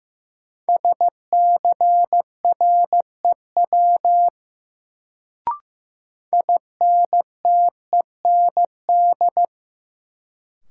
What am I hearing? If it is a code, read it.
Morse code, heard as a keyed tone, about 15 wpm, 700 Hz. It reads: SCREW INTEND
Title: Sentences from Top 1600 Words - New Word Drill - 15wpm